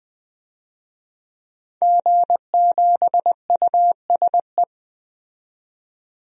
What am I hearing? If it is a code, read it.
Morse code, heard as a keyed tone, about 20 wpm, 700 Hz. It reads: G7USE
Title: DXCC-100 Call Signs - 5 Characters - 20wpm